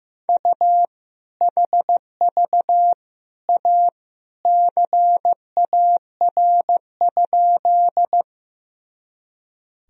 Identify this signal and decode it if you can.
Morse code, heard as a keyed tone, about 16 wpm, 700 Hz. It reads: U HV A CAR?
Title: Sentences from Top 300 Words and Abbreviations 15wpm